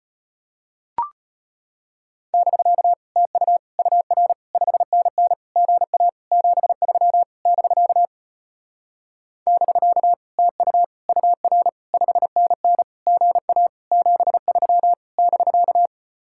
Morse code, heard as a keyed tone, about 33 wpm, 700 Hz. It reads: <BK> TU UR 5NN GA 73 <BK> <BK> TU UR 5NN GA 73 <BK>